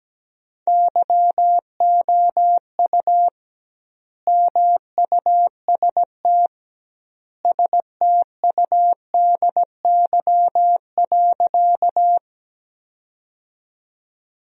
Morse code, heard as a keyed tone, about 17 wpm, 700 Hz. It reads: YOU MUST STUDY.
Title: 2x Word-Spacing - Sentences from Top 300 Words - 17wpm